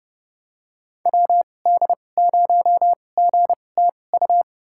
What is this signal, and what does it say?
Morse code, heard as a keyed tone, about 31 wpm, spaced as for 22 wpm, 700 Hz. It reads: WD0GTU
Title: USA Call Signs - 6 Characters - 2x Character Spacing - 30wpm